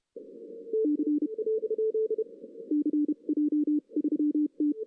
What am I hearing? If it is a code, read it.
Morse code, heard as a keyed tone, about 32 wpm, spaced as for 27 wpm, 300 Hz. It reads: C CJ3T
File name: 2016 arrl dx cw / 2016-ARRL-DX-CW-K5TR- - 0-3